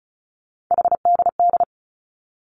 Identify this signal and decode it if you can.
Morse code, heard as a keyed tone, about 36 wpm, 700 Hz. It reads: HDD
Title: Three Challenging Letter-Number Combinations 35wpm